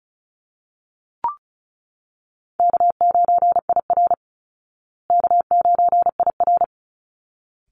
Morse code, heard as a keyed tone, about 34 wpm, 700 Hz. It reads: K9IR K9IR
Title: Easy Call Signs Repeated 2x - 35wpm